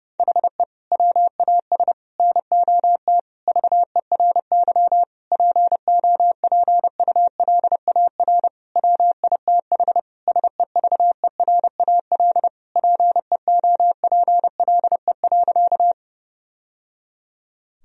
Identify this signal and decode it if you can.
Morse code, heard as a keyed tone, about 30 wpm, 700 Hz. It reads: HE WAS NOT VERY POPULAR WITH SEVERAL PEOPLE.
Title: Sentences from Top 1000 Words - Review Part 3 - 30wpm